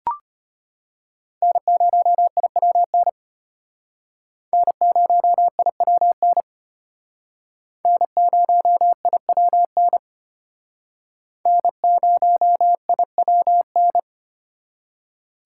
Morse code, heard as a keyed tone, about 34 wpm, 700 Hz. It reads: N0IWN N0IWN N0IWN N0IWN